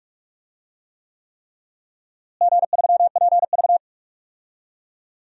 Morse code, heard as a keyed tone, about 42 wpm, 700 Hz. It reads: G3PV
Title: DXCC-100 Call Signs - 4 Characters - Repeated 3x - 45wpm